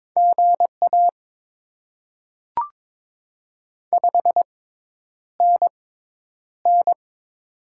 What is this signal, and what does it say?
Morse code, heard as a keyed tone, about 22 wpm, spaced as for 10 wpm, 700 Hz. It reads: GA 5NN
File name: Top 20 QSO Elements - 6x Character Spacing - 22wpm